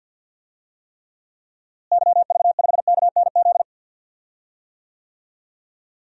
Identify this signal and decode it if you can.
Morse code, heard as a keyed tone, about 51 wpm, 700 Hz. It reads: YV5KNZ
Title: DXCC-100 Call Signs - 6 Characters - Repeated 3x - 50wpm